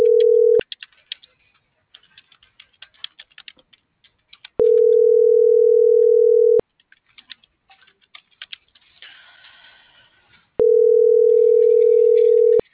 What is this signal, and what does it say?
Ringback tone (North America), 440 + 480 Hz.